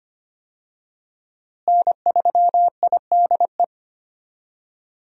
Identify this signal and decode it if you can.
Morse code, heard as a keyed tone, about 25 wpm, 700 Hz. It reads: N3IDE
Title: USA Call Signs - 5 Characters - Repeated 2x - 25wpm